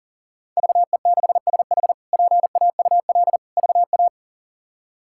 Morse code, heard as a keyed tone, about 37 wpm, 700 Hz. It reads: VE6SH PAUL VA
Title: K1USN SST QSO Practice - 40wpm